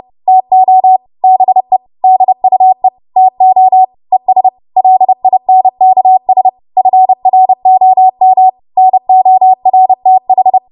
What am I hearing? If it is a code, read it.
Morse code, heard as a keyed tone, about 30 wpm, 750 Hz. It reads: TO BE DUE TO ES LINKS FROM NORTH
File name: Morse / 30WPM